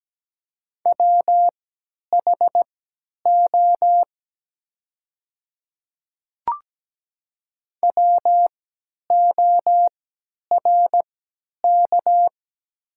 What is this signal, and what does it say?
Morse code, heard as a keyed tone, about 17 wpm, spaced as for 10 wpm, 700 Hz. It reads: WHO WORK
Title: 3x Character Spacing - Top 500 Words - 17wpm